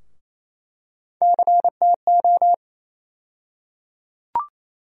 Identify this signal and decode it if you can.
Morse code, heard as a keyed tone, about 28 wpm, 700 Hz. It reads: CTO
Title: USA Call Sign Postfixes - 3 Letters 28wpm